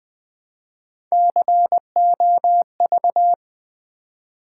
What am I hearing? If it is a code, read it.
Morse code, heard as a keyed tone, about 21 wpm, 700 Hz. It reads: COV
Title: Build A Word - 1 Letter At A Time - 3 to 5-Letter Words - 20wpm